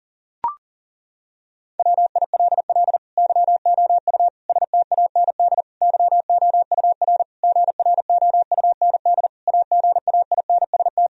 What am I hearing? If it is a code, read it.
Morse code, heard as a keyed tone, about 37 wpm, 700 Hz. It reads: WILL YOU STAND YOUR GROUND AGAINST
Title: Sentences from Top 1000 Words - Review Part 2 - 40wpm